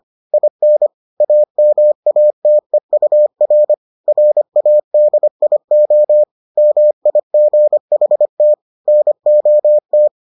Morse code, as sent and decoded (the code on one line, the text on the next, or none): .. -. / .- -- .- - . ..- .-. / .-. .- -.. .. --- / -- .. --. .... - / -. --- -
IN AMATEUR RADIO MIGHT NOT